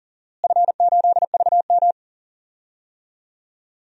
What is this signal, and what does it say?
Morse code, heard as a keyed tone, about 40 wpm, 700 Hz. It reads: F8VM
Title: DXCC-100 Call Signs - 4 Characters - Repeated 3x - 40wpm